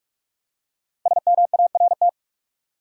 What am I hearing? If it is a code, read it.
Morse code, heard as a keyed tone, about 46 wpm, 700 Hz. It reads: SMART
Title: Five Letter Words - 45wpm